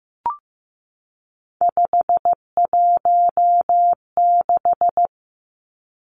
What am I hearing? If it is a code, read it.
Morse code, heard as a keyed tone, about 15 wpm, 700 Hz. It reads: 516
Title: Three Challenging Letter-Number Combinations 15wpm